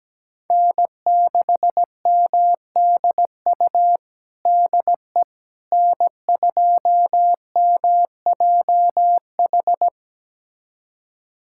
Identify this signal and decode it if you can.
Morse code, heard as a keyed tone, about 17 wpm, 700 Hz. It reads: N6MDU DE N2MJH
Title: USA Callsign DE Callsign - 5 Characters - 17wpm